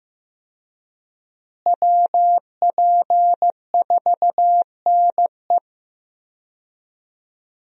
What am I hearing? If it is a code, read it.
Morse code, heard as a keyed tone, about 15 wpm, 700 Hz. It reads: WP4NE
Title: USA Call Signs - 5 Characters - Repeated 2x - 15wpm